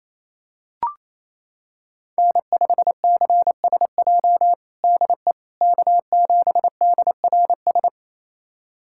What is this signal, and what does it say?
Morse code, heard as a keyed tone, about 27 wpm, 700 Hz. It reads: N5CSJ DE K7DRS
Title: USA Callsign DE Callsign - 5 Characters - 28wpm